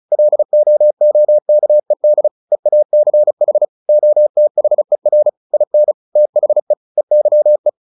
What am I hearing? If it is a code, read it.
Morse code, heard as a keyed tone, about 34 wpm, 600 Hz. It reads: LOOKED EACH OTHER IN THE EYE
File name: the old man and the sea 2 35wpm